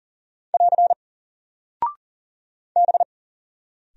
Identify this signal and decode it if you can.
Morse code, heard as a keyed tone, about 40 wpm, 700 Hz. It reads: <AR> B
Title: Single Character (Letter:Number:Punctuation:Prosign) 40wpm